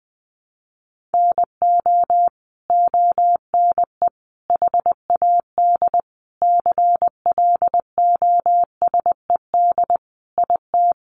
Morse code, heard as a keyed tone, about 20 wpm, 700 Hz. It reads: NO ONE HAD CLOSED IT
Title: Sentences from Top 1400 Words 20wpm